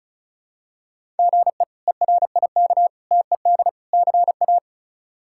Morse code, heard as a keyed tone, about 36 wpm, 700 Hz. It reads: GE ERIK TED CA
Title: K1USN SST QSO Practice - 35wpm